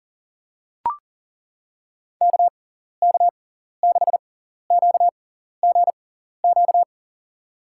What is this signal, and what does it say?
Morse code, heard as a keyed tone, about 39 wpm, spaced as for 14 wpm, 700 Hz. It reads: KK6QGQ
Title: USA Call Signs - 6 Characters - 6x Character Spacing - 40wpm